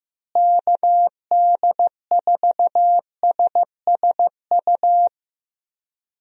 Morse code, heard as a keyed tone, about 15 wpm, 700 Hz. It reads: KD4SSU
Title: USA Call Signs - 6 Characters - Repeated 3x - 15wpm